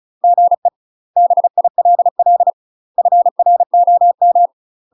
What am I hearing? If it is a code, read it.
Morse code, heard as a keyed tone, about 35 wpm, 700 Hz. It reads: GE BILL FROM
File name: UK MASTER QSOs TRAINING/35WPM